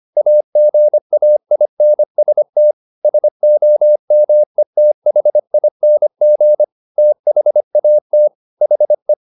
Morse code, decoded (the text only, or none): AGAINST SOMETHING THAT HE